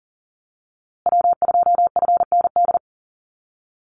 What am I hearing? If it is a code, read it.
Morse code, heard as a keyed tone, about 40 wpm, 700 Hz. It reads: W2FND
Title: USA Call Signs Repeated 2x 40wpm